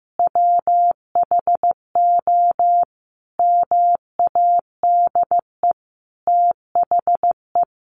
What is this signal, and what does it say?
Morse code, heard as a keyed tone, about 15 wpm, 700 Hz. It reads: WHO MADE THE
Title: Sentences from Top 200 Words 15wpm